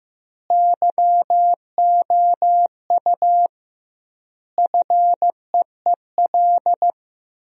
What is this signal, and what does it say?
Morse code, heard as a keyed tone, about 16 wpm, 700 Hz. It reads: YOU FEEL
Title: 2x Word-Spacing - Sentences from Top 400 Words - 15wpm